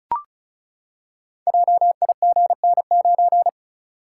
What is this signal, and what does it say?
Morse code, heard as a keyed tone, about 36 wpm, 700 Hz. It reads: JIGN9
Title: Code Groups - 5 Letter:Numbers - 35wpm